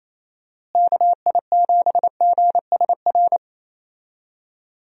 Morse code, heard as a keyed tone, about 27 wpm, 700 Hz. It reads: KI7GSR
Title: USA Call Signs - 6 Characters - Repeated 3x - 28wpm